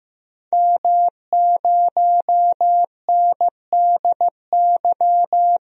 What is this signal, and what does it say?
Morse code, heard as a keyed tone, about 15 wpm, 700 Hz. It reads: M0NDY
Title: DXCC-100 Call Signs - 5 Characters - Repeated 3x - 15wpm